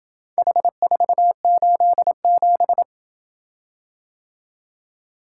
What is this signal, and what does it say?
Morse code, heard as a keyed tone, about 27 wpm, 700 Hz. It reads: H487